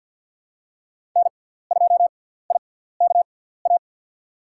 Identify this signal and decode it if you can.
Morse code, heard as a keyed tone, about 47 wpm, spaced as for 17 wpm, 700 Hz. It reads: N2IKA
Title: USA Call Signs - 6x Character Spacing - 50wpm